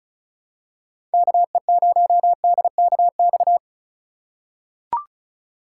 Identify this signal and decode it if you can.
Morse code, heard as a keyed tone, about 36 wpm, 700 Hz. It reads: KE0DKX
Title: USA Call Signs - 6 Characters - 3x Character Spacing - 35wpm